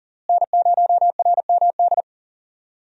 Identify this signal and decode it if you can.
Morse code, heard as a keyed tone, about 40 wpm, 700 Hz. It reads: N0RMD
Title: USA Call Signs - 5 Characters - Repeated 3x - 40wpm